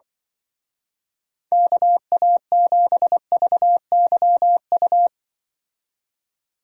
Morse code, heard as a keyed tone, about 25 wpm, 700 Hz. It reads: KA7VYU
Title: Speed Racing - USA 2x3 Call Signs 20wpm